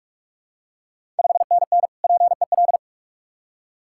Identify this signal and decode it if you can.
Morse code, heard as a keyed tone, about 46 wpm, 690 Hz. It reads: 5NN PEL
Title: Ontario QSO Party Exchange 45wpm